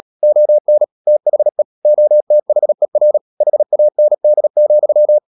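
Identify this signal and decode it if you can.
Morse code, heard as a keyed tone, about 38 wpm, 600 Hz. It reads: ON THE OTHER HAND,
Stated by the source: Morse code practice at 35–39 wpm sending English text of Ken Follett's novel Never, from CW med Gnister.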